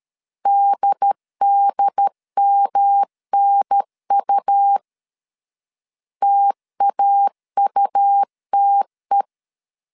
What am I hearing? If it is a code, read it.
Morse code, heard as a keyed tone, about 13 wpm, 800 Hz. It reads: DDMNU TAUTE
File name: CW / DET / 1 3 1500C S B 12wpm 2828